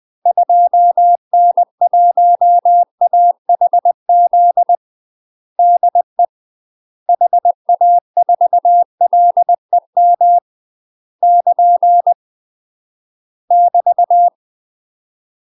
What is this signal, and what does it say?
Morse code, heard as a keyed tone, about 20 wpm, 700 Hz. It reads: 2N1AHZ DE HA4LEM <KN> <BT>
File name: UK MASTER QSOs TRAINING / 20WPM / UK MASTER QSOs 0069